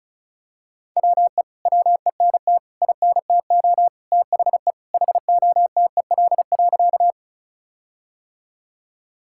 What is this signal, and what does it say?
Morse code, heard as a keyed tone, about 36 wpm, 700 Hz. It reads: WE WENT INTO THE HOTEL.
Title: Sentences from Top 1600 Words 35wpm